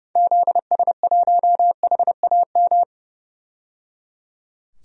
Morse code, heard as a keyed tone, about 30 wpm, 700 Hz. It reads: ZS1HAM
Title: DXCC-100 Call Signs - 6 Characters - 30wpm